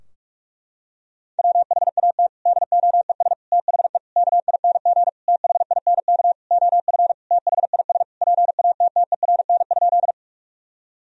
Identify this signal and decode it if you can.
Morse code, heard as a keyed tone, about 46 wpm, 700 Hz. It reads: WHAT DOES THE KING THINK OF THIS PATTERN?
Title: Sentences from Top 1000 Words - Review Part 2 - 45wpm